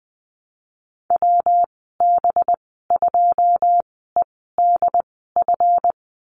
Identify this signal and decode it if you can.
Morse code, heard as a keyed tone, about 20 wpm, spaced as for 14 wpm, 700 Hz. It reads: WB2EDF
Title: 200 Lesson 32g - Letter k - Call Signs 20wpm